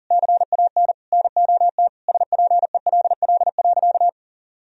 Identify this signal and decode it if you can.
Morse code, heard as a keyed tone, about 40 wpm, 700 Hz. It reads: CAN NOT SPELL.
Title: Sentences from Top 1000 Words - Review Part 1 - 40wpm